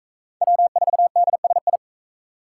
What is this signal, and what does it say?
Morse code, heard as a keyed tone, about 40 wpm, 700 Hz. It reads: W4DSI